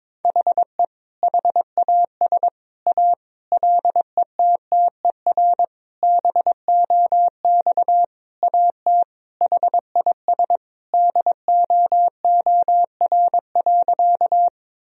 Morse code, heard as a keyed tone, about 22 wpm, 700 Hz. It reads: HE HAS A LETTER BOX AT HIS DOOR.
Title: Sentences from Top 1000 Words - Review Part 2 - 22wpm